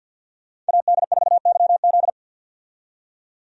Morse code, heard as a keyed tone, about 47 wpm, 700 Hz. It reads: AD4YZ
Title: USA Call Signs - 5 Characters - 50wpm